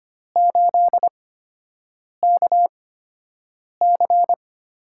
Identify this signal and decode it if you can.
Morse code, heard as a keyed tone, about 25 wpm, spaced as for 7 wpm, 700 Hz. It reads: 8KC
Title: USA Call Signs - 6 Characters - 8x Character Spacing - 25wpm